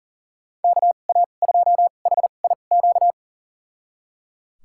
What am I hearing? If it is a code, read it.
Morse code, heard as a keyed tone, about 37 wpm, spaced as for 28 wpm, 700 Hz. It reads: KA2HIQ
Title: USA Call Signs - 6 Characters - 2x Character Spacing - 40wpm